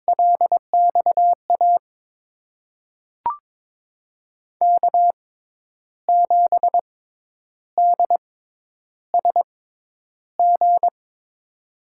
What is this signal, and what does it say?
Morse code, heard as a keyed tone, about 23 wpm, spaced as for 10 wpm, 700 Hz. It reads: LXA K7DSG
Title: USA Call Signs - 5 Characters - 6x Character Spacing - 22wpm